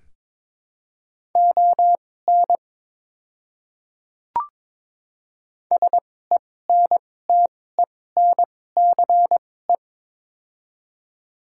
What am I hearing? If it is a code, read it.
Morse code, heard as a keyed tone, about 21 wpm, spaced as for 16 wpm, 700 Hz. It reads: ON SENTENCE
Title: 2x Character Spacing - Top 200 Words - 22wpm